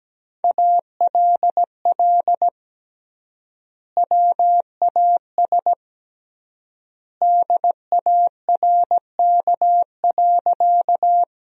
Morse code, heard as a keyed tone, about 17 wpm, 700 Hz. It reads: ALL WAS DARK.
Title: 3x Word-Spacing - Sentences from Top 500 Words - 17wpm